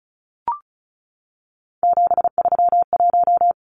Morse code, heard as a keyed tone, about 34 wpm, 700 Hz. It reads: 731
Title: Numbers - 3-Digit 35wpm